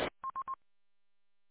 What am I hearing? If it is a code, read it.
There is no code.